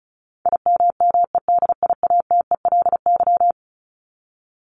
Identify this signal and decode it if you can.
Morse code, heard as a keyed tone, about 34 wpm, 700 Hz. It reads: IMMEDIATELY
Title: LY Suffix - Top 100 Words with Base Words 35wpm